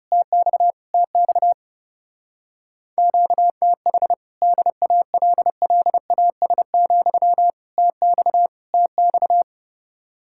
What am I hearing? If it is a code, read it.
Morse code, heard as a keyed tone, about 40 wpm, 700 Hz. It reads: TX TX QTH DALLAS, TX TX